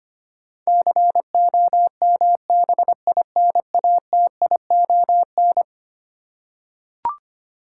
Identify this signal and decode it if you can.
Morse code, heard as a keyed tone, about 25 wpm, 700 Hz. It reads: COMBINATION
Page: morse-code-ninja-practice-25wpm